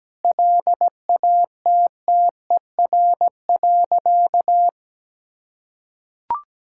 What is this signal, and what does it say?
Morse code, heard as a keyed tone, about 17 wpm, 700 Hz. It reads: LATTER.
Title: Sentences from Top 1600 Words 17wpm